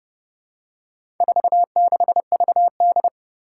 Morse code, heard as a keyed tone, about 30 wpm, 700 Hz. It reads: 46VD
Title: Four Challenging Letter-Number Combinations 30wpm